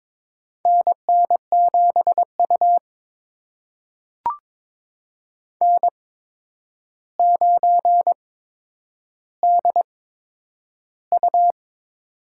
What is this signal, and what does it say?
Morse code, heard as a keyed tone, about 22 wpm, spaced as for 9 wpm, 700 Hz. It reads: NN7U N9DU
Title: USA Call Signs - 4 Characters - 8x Character Spacing - 22wpm